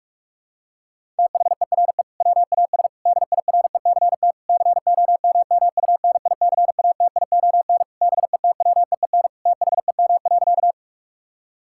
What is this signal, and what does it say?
Morse code, heard as a keyed tone, about 46 wpm, 700 Hz. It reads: THERE WAS DIRECT COMMUNICATION BETWEEN THEM.
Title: Sentences from Top 1000 Words - Review Part 6 - 45wpm